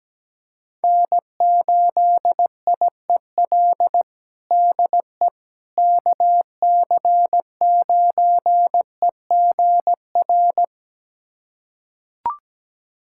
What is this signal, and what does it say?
Morse code, heard as a keyed tone, about 17 wpm, 700 Hz. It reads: N8IEL DE KC9EGR
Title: USA Callsign DE Callsign 17wpm